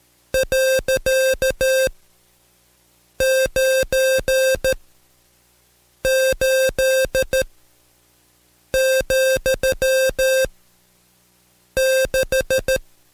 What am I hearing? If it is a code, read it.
Morse code, heard as a keyed tone, about 13 wpm, spaced as for 5 wpm, 530 Hz. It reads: .98,6
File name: Code Quick / AudioDownld2 / Seg107-Day13-DayDrill 13